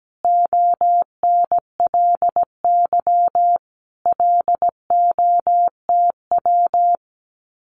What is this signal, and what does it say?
Morse code, heard as a keyed tone, about 17 wpm, 700 Hz. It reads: ONLY LOTW
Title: Common QSO Phrases Encore 17wpm